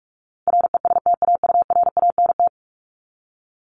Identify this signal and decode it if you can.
Morse code, heard as a keyed tone, about 44 wpm, 700 Hz. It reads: RESTAURANT
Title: Top 1751-2000 Words 45wpm